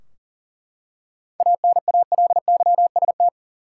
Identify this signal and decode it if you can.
Morse code, heard as a keyed tone, about 39 wpm, 700 Hz. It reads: ANALYST
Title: Practice QSO - Exchange 3 - More Info - 40wpm